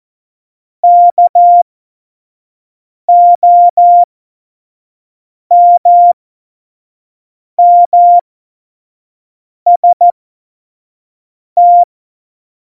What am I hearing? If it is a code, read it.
Morse code, heard as a keyed tone, about 14 wpm, spaced as for 5 wpm, 700 Hz. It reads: KOMMST
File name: Morsekurs / Witze / Witz 10.Juli 2018, 25 F70